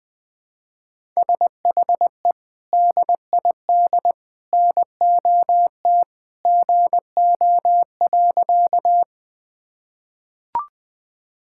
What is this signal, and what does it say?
Morse code, heard as a keyed tone, about 20 wpm, 700 Hz. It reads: SHE DID NOT GO.
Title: Sentences with 3-Word Phrases - Spoken Beginning - 20wpm